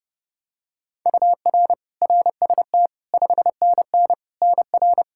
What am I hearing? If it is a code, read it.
Morse code, heard as a keyed tone, about 30 wpm, 700 Hz. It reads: UR RST 5NN NR
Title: K1USN SST QSO Practice 30wpm